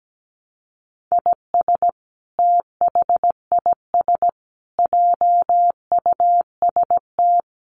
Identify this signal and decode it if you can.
Morse code, heard as a keyed tone, about 17 wpm, 700 Hz. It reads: IS THIS JUST
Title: Sentences from Top 1100 Words 17wpm